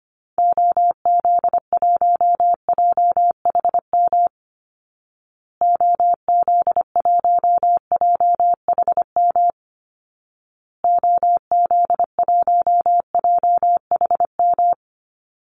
Morse code, heard as a keyed tone, about 25 wpm, 700 Hz. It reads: OZ1JHM OZ1JHM OZ1JHM
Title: Call Signs Repeated 2x - 25wpm